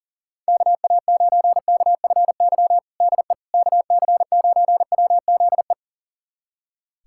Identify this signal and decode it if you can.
Morse code, heard as a keyed tone, about 40 wpm, 700 Hz. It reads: KA9KFY DE KC9WZE
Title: USA Callsign DE Callsign - 6 Characters - 40wpm